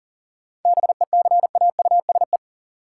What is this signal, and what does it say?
Morse code, heard as a keyed tone, about 40 wpm, 700 Hz. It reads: BECAUSE
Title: Sentences from Top 600 Words - New Word Drill - 40wpm